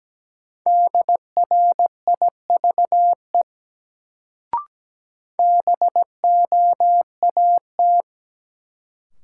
Morse code, heard as a keyed tone, about 17 wpm, 700 Hz. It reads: DRIVE BOAT
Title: Sentences from Top 500 Words - New Word Drill - 17wpm